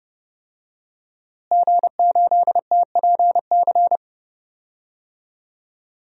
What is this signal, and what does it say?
Morse code, heard as a keyed tone, about 31 wpm, 700 Hz. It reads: G8TPC
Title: DXCC-100 Call Signs - 5 Characters - Repeated 3x - 30wpm